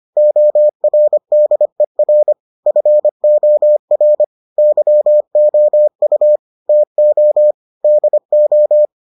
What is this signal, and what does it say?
Morse code, heard as a keyed tone, about 25 wpm, 600 Hz. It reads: ORDER FOR YOU TO DO